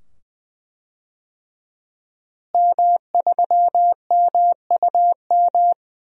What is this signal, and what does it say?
Morse code, heard as a keyed tone, about 20 wpm, 700 Hz. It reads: M3MUM
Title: DXCC-100 Call Signs - 5 Characters - Repeated 2x - 20wpm